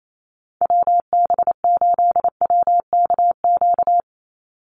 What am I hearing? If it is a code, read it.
Morse code, heard as a keyed tone, about 28 wpm, 700 Hz. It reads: WB8WKQ
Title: Call Signs - 28wpm